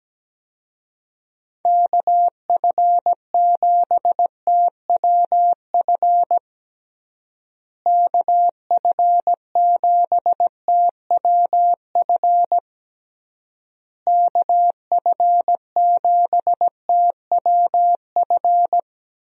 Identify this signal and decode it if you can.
Morse code, heard as a keyed tone, about 17 wpm, 700 Hz. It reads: KF7TWF KF7TWF KF7TWF
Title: USA Call Signs - 6 Characters - Repeated 3x - 17wpm